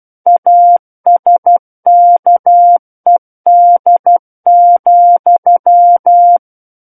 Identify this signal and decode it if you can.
Morse code, heard as a keyed tone, about 12 wpm, 700 Hz. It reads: ASKED,
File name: Book One / 03-Chapter 3 - 12 WPM